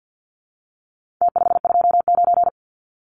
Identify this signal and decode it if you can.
Morse code, heard as a keyed tone, about 50 wpm, 700 Hz. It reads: T529
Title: Numbers - Contest Style - 4-Digit 50wpm